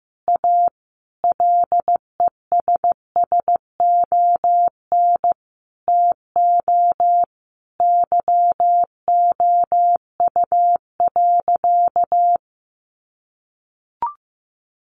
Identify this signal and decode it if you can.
Morse code, heard as a keyed tone, about 15 wpm, 700 Hz. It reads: A LESSON TO YOU.
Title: Sentences from Top 1200 Words 15wpm